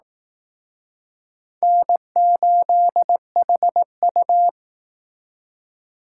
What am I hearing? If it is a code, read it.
Morse code, heard as a keyed tone, about 19 wpm, 700 Hz. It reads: N8HU